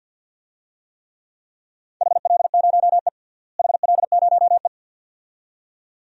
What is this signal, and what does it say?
Morse code, heard as a keyed tone, about 50 wpm, 700 Hz. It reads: HL0E HL0E
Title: DXCC-100 Call Signs - 4 Characters - Repeated 2x - 50wpm